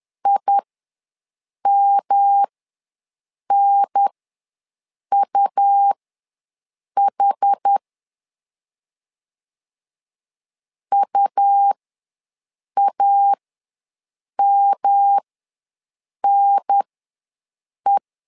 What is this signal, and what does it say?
Morse code, heard as a keyed tone, about 10 wpm, spaced as for 6 wpm, 800 Hz. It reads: IMNUH UAMNE
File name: LEZIONI / 3 MWRO / 3 M ws 10wpm 2121